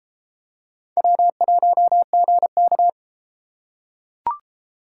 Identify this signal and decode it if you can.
Morse code, heard as a keyed tone, about 30 wpm, 700 Hz. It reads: W1GK